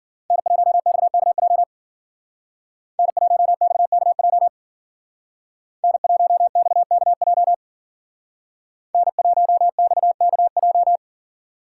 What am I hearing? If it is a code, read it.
Morse code, heard as a keyed tone, about 49 wpm, 700 Hz. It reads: N1XKJ N1XKJ N1XKJ N1XKJ
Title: Speed Racing - USA 1x3 Call Signs 40wpm